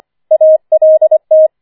Morse code, heard as a keyed tone, about 25 wpm, 630 Hz. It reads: ALT
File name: 25wpm